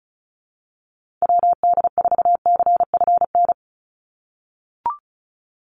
Morse code, heard as a keyed tone, about 34 wpm, 700 Hz. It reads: WD4CFN
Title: NA SOTA Chaser Callsigns 35wpm